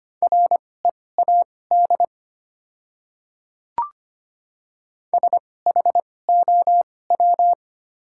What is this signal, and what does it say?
Morse code, heard as a keyed tone, about 25 wpm, spaced as for 18 wpm, 700 Hz. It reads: READ SHOW
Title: 2x Character Spacing - Top 200 Words - 25wpm